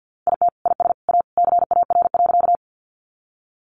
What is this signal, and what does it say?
Morse code, heard as a keyed tone, about 50 wpm, 700 Hz. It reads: IT IS A CAR.
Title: Easy Phrases 50wpm